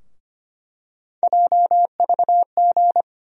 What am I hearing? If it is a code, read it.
Morse code, heard as a keyed tone, about 25 wpm, 700 Hz. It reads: JVG
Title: USA Call Sign Postfixes - 3 Letters 25wpm